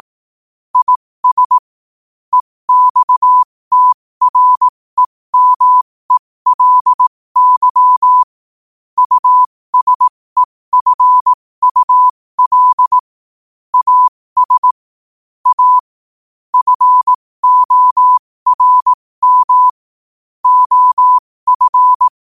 Morse code, heard as a keyed tone, about 18 wpm, spaced as for 15 wpm, 1000 Hz. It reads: IS EXTREMELY USEFUL AS A FORM OF
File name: qotdfiles / 15wpm